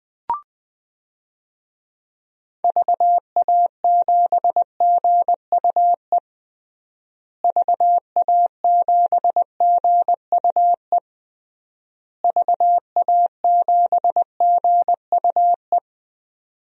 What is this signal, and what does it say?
Morse code, heard as a keyed tone, about 21 wpm, 700 Hz. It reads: VA7GUE VA7GUE VA7GUE
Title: DXCC-100 Call Signs - 6 Characters - Repeated 3x - 20wpm